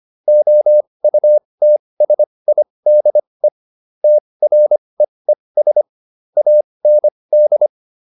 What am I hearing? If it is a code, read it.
Morse code, heard as a keyed tone, about 25 wpm, spaced as for 20 wpm, 600 Hz. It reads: OUTSIDE TREES AND